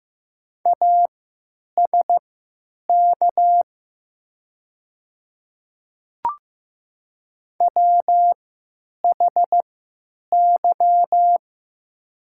Morse code, heard as a keyed tone, about 15 wpm, spaced as for 9 wpm, 700 Hz. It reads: ASK WHY
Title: 3x Character Spacing - Top 200 Words - 15wpm